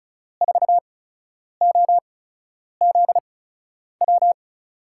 Morse code, heard as a keyed tone, about 34 wpm, spaced as for 10 wpm, 700 Hz. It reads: 4OZW
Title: USA Call Signs - 6 Characters - 8x Character Spacing - 35wpm